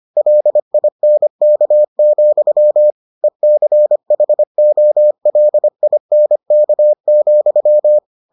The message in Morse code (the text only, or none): LINK, ECHOLINK,